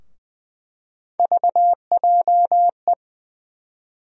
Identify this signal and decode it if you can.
Morse code, heard as a keyed tone, about 20 wpm, 700 Hz. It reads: VJE
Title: USA Call Sign Postfixes - 3 Letters 20wpm